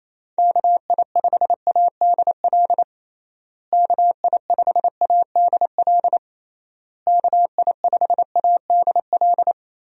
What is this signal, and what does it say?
Morse code, heard as a keyed tone, about 28 wpm, 700 Hz. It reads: KI5ADL KI5ADL KI5ADL
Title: USA Call Signs - 6 Characters - Repeated 3x - 28wpm